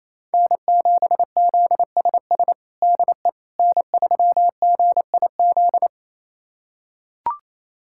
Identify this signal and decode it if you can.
Morse code, heard as a keyed tone, about 27 wpm, 700 Hz. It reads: N7ZSS DE N3GIZ
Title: USA Callsign DE Callsign - 5 Characters - 28wpm